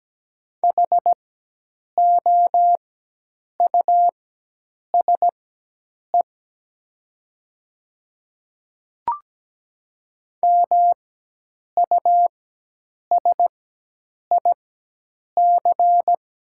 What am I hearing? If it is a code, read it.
Morse code, heard as a keyed tone, about 17 wpm, spaced as for 8 wpm, 700 Hz. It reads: HOUSE MUSIC
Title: 4x Character Spacing - Top 300 Words - 17wpm